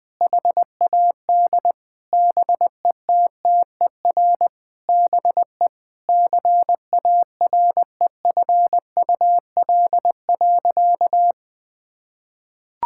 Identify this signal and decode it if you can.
Morse code, heard as a keyed tone, about 19 wpm, 700 Hz. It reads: HAD BETTER BE CAREFUL.
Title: Sentences from Top 1600 Words 20wpm